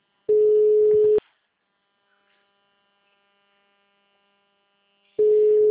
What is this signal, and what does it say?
Ringback tone (Europe, 425 Hz).